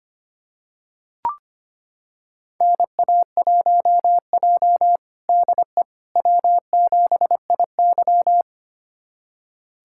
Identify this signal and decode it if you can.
Morse code, heard as a keyed tone, about 25 wpm, 700 Hz. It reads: NA1J DE W7IY
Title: USA Callsign DE Callsign - 4 Characters - 25wpm